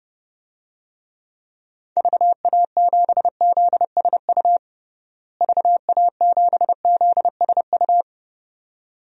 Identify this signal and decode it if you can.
Morse code, heard as a keyed tone, about 31 wpm, 700 Hz. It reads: VA7ZSU VA7ZSU
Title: DXCC-100 Call Signs - 6 Characters - Repeated 2x - 30wpm